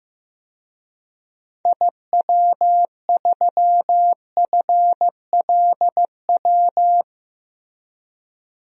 Morse code, heard as a keyed tone, about 15 wpm, 700 Hz. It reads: IW3FLW
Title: DXCC-100 Call Signs - 6 Characters - Repeated 3x - 15wpm